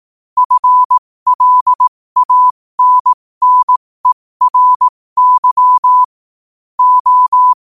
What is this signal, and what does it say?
Morse code, heard as a keyed tone, about 18 wpm, spaced as for 15 wpm, 1000 Hz. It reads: FLANNERY O